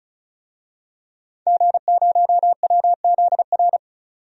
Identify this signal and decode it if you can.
Morse code, heard as a keyed tone, about 36 wpm, 700 Hz. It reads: G0WZR
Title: DXCC-100 Call Signs - 5 Characters - 35wpm